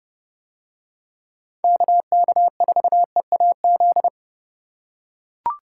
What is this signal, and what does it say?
Morse code, heard as a keyed tone, about 31 wpm, 700 Hz. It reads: KK4EAZ